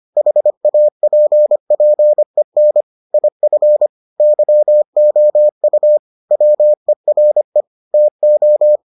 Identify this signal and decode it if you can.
Morse code, heard as a keyed tone, about 25 wpm, 600 Hz. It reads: HAPPEN IF YOU WERE TO